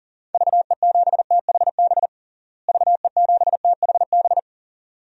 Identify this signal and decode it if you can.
Morse code, heard as a keyed tone, about 40 wpm, 700 Hz. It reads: VE7THB VE7THB
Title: DXCC-100 Call Signs - 6 Characters - Repeated 2x - 40wpm